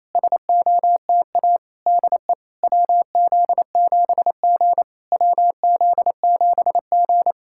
Morse code, heard as a keyed tone, about 28 wpm, 700 Hz. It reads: SOTA DE WZ7G WZ7G
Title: SOTA QSO Practice 28wpm